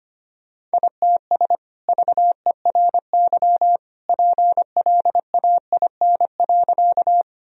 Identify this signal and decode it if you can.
Morse code, heard as a keyed tone, about 25 wpm, 700 Hz. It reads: ITS VERY PLAIN.
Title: Sentences from Top 1000 Words - Review Part 2 - 25wpm